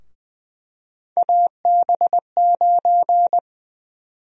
Morse code, heard as a keyed tone, about 20 wpm, 700 Hz. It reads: AB9
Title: USA Call Sign Prefixes - 20wpm